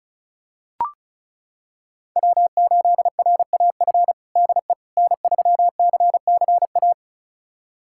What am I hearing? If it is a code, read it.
Morse code, heard as a keyed tone, about 36 wpm, 700 Hz. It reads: W8RAF DE N3CCA
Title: USA Callsign DE Callsign - 5 Characters - 35wpm